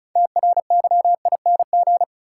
Morse code, heard as a keyed tone, about 36 wpm, 700 Hz. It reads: TRYING
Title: Sets of 2 Words - New Word Drill - 35wpm